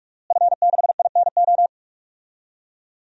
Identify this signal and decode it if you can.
Morse code, heard as a keyed tone, about 46 wpm, 700 Hz. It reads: F6INO
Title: DXCC-100 Call Signs - 5 Characters - Repeated 2x - 45wpm